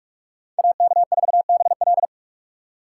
Morse code, heard as a keyed tone, about 46 wpm, 700 Hz. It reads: AK4BL
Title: USA Call Signs - 5 Characters - 3x Character Spacing - 45wpm